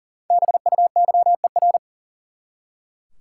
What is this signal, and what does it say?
Morse code, heard as a keyed tone, about 40 wpm, 700 Hz. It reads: BUYER
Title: Five Letter Words - 40wpm